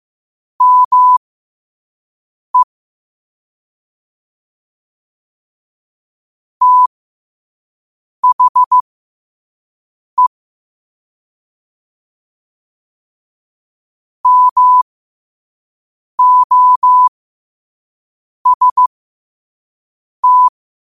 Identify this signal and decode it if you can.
Morse code, heard as a keyed tone, about 15 wpm, spaced as for 5 wpm, 1000 Hz. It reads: ME THE MOST
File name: qotdfiles / 5wpm